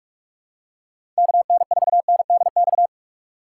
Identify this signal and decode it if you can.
Morse code, heard as a keyed tone, about 43 wpm, 700 Hz. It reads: KN4NDX
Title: USA Call Signs - 6 Characters - 4x Character Spacing - 45wpm